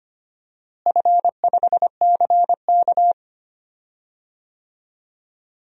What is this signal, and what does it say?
Morse code, heard as a keyed tone, about 25 wpm, 700 Hz. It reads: F5CK
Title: DXCC-100 Call Signs - 4 Characters - Repeated 2x - 25wpm